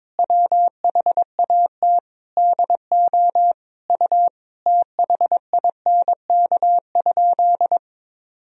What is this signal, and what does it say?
Morse code, heard as a keyed tone, about 22 wpm, 700 Hz. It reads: WHAT DO U THINK?
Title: Sentences from Top 200 Words and Abbreviations 22wpm